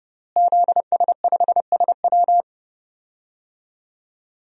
Morse code, heard as a keyed tone, about 30 wpm, 700 Hz. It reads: ZS5SW
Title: DXCC-100 Call Signs - 5 Characters - Repeated 2x - 30wpm